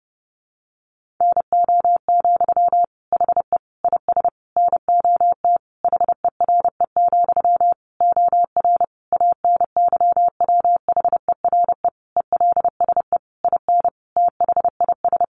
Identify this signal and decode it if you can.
Morse code, heard as a keyed tone, about 30 wpm, 700 Hz. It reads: NO, HE IS NOT HERE, OR ANYWHERE ELSE IN THIS
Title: Sentences from Top 1400 Words 30wpm